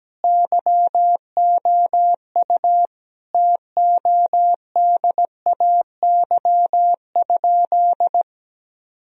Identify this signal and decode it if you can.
Morse code, heard as a keyed tone, about 17 wpm, 700 Hz. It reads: YOU TODAY?
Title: Sentences from Top 1000 Words - Review Part 3 - 17wpm